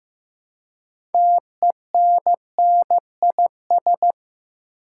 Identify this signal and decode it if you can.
Morse code, heard as a keyed tone, about 15 wpm, 700 Hz. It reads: TENNIS
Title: Six Letter Words - 15wpm